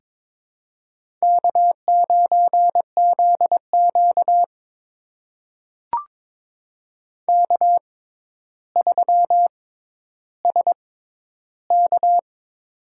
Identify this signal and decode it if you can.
Morse code, heard as a keyed tone, about 22 wpm, spaced as for 11 wpm, 700 Hz. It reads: K9ZQ K3SK
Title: USA Call Signs - 4 Characters - 6x Character Spacing - 22wpm